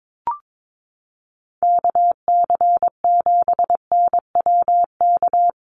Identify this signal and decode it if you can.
Morse code, heard as a keyed tone, about 22 wpm, 700 Hz. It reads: KC7NWK
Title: USA Call Signs Repeated 3x 22wpm